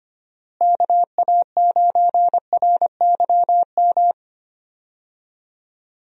Morse code, heard as a keyed tone, about 25 wpm, 700 Hz. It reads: KA9RYM
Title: USA Call Signs - 6 Characters - Repeated 3x - 25wpm